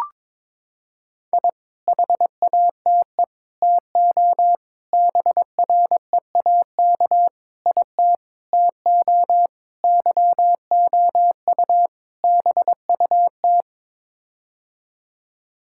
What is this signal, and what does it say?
Morse code, heard as a keyed tone, about 22 wpm, 700 Hz. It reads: I HATE TO BREAK IT TO YOU BUT
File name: practice / 22wpm / Sets of 8 Words - 22wpm